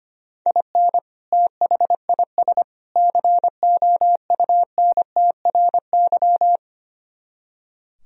Morse code, heard as a keyed tone, about 25 wpm, 700 Hz. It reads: IN THIS COUNTRY
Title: Speed-Racing - Sets of 3 Words 25wpm